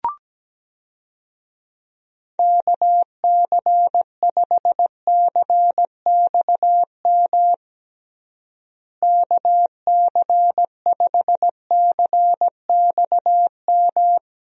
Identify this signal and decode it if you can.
Morse code, heard as a keyed tone, about 17 wpm, 700 Hz. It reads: KC5CXM KC5CXM
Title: USA Call Signs - 6 Characters - Repeated 3x - 17wpm